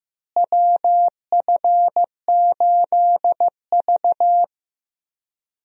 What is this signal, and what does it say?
Morse code, heard as a keyed tone, about 15 wpm, 700 Hz. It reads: WF8V
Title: USA Call Signs - 4 Characters - Repeated 3x - 15wpm